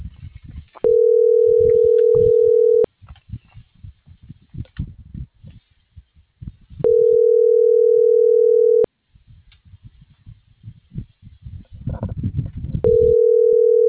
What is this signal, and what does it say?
Ringback tone (North America), 440 + 480 Hz.